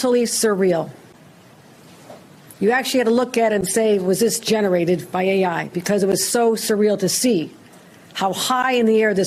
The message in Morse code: none